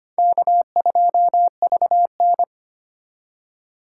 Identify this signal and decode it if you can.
Morse code, heard as a keyed tone, about 25 wpm, 700 Hz. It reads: K2VN
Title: USA Call Signs - 8x Character Spacing - 25wpm